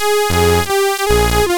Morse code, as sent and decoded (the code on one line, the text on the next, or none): none